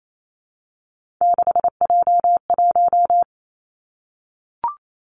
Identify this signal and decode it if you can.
Morse code, heard as a keyed tone, about 28 wpm, 700 Hz. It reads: 6J1
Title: Three Challenging Letter-Number Combinations 28wpm